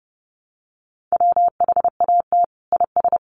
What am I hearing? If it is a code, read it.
Morse code, heard as a keyed tone, about 30 wpm, 700 Hz. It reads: WHAT IS
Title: Sets of 2 Words 30wpm